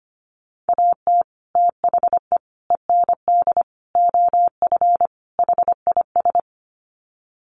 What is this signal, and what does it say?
Morse code, heard as a keyed tone, about 25 wpm, 700 Hz. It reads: AT THE END OF HIS
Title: Sets of 5 Words Encore 25wpm